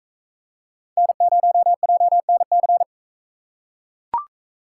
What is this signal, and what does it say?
Morse code, heard as a keyed tone, about 40 wpm, 700 Hz. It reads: N0JNC